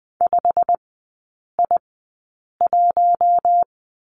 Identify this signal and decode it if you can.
Morse code, heard as a keyed tone, about 20 wpm, spaced as for 8 wpm, 700 Hz. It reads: 5I1
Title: 114 Lesson 19e - Letter w - Three Characters 20wpm